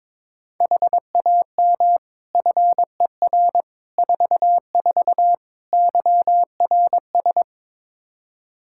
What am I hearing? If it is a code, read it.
Morse code, heard as a keyed tone, about 22 wpm, 700 Hz. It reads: HAM FER 44 YRS
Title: Practice QSO - Exchange 3 - More Info - 22wpm